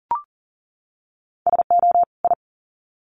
Morse code, heard as a keyed tone, about 39 wpm, 700 Hz. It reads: SO I
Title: Sets of 2 Words 40wpm